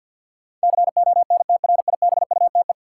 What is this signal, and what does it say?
Morse code, heard as a keyed tone, about 52 wpm, 700 Hz. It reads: CONTRIBUTE
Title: Sentences from Top 1400 Words - New Word Drill - 50wpm